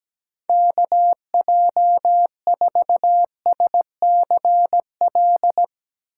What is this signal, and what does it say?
Morse code, heard as a keyed tone, about 17 wpm, 700 Hz. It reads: KJ4SCL
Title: USA Call Signs - 6 Characters - 3x Character Spacing - 17wpm